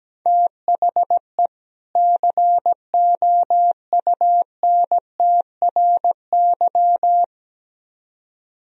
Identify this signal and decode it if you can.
Morse code, heard as a keyed tone, about 17 wpm, 700 Hz. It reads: THE COUNTRY
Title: Speed-Racing - Sets of 2 Words 17wpm